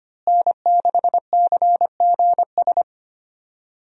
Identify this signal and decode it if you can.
Morse code, heard as a keyed tone, about 25 wpm, 700 Hz. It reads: N6CGS
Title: USA Call Signs - 5 Characters - Repeated 3x - 25wpm